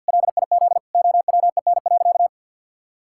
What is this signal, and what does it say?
Morse code, heard as a keyed tone, about 50 wpm, 700 Hz. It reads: LIZ OWEN.
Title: Pangram Challenge v2 - 50wpm